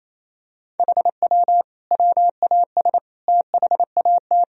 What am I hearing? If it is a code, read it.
Morse code, heard as a keyed tone, about 28 wpm, 700 Hz. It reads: HW WAS THAT
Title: Sentences from Top 500 Words and Abbreviations 28wpm